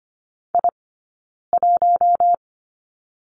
Morse code, heard as a keyed tone, about 25 wpm, 700 Hz. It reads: I 1
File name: course / 25:10wpm / 092 Lesson 16d - Letter c - Two Characters 25wpm